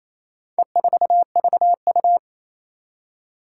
Morse code, heard as a keyed tone, about 28 wpm, 700 Hz. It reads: E4VU
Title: Four Challenging Letter-Number Combinations 28wpm